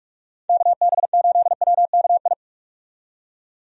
Morse code, heard as a keyed tone, about 46 wpm, 700 Hz. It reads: KB8WKI
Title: USA Call Signs - 6 Characters - 45wpm